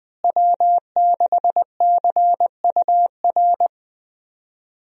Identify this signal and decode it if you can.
Morse code, heard as a keyed tone, about 20 wpm, 700 Hz. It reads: W6CUR
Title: USA Call Signs - 5 Characters - Repeated 2x - 20wpm